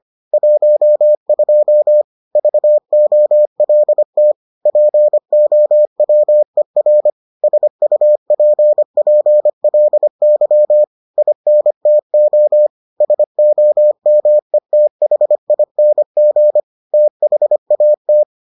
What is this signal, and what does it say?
Morse code, heard as a keyed tone, about 25 wpm, 600 Hz. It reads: 12 VOLT POWER SUPPLY INTO SOMETHING THAT